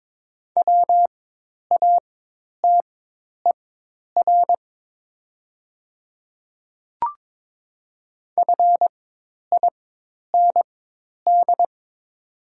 Morse code, heard as a keyed tone, about 22 wpm, spaced as for 10 wpm, 700 Hz. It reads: WATER FIND
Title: 4x Character Spacing - Top 200 Words - 22wpm